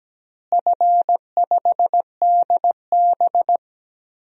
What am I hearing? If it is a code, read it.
Morse code, heard as a keyed tone, about 17 wpm, 700 Hz. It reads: F5DB
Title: DXCC-100 Call Signs - 4 Characters - Repeated 2x - 17wpm